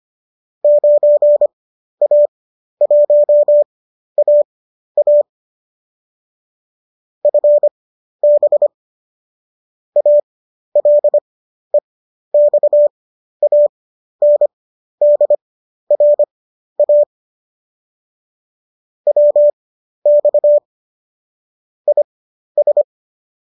Morse code, heard as a keyed tone, about 25 wpm, spaced as for 12 wpm, 590 Hz. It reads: 9A1AA FB ALEXANDRA WX IS
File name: qso012 12